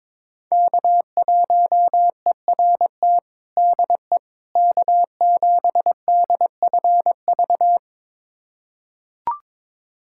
Morse code, heard as a keyed tone, about 22 wpm, 700 Hz. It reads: K1ERT DE K7DFV
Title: USA Callsign DE Callsign - 5 Characters - 22wpm